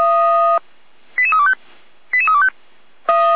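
Five-tone selective call (ZVEI), heard as quick runs of five tones, tones of about 70 ms. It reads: 88325 88325
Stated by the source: ZVEI five-tone selective call received on an SDR.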